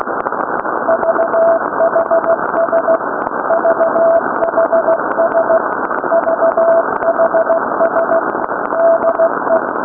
Morse code, heard as a keyed tone, about 16 wpm, 660 Hz, with noise. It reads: VHS VHS VHS DE
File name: VCRLDE / VHS 3510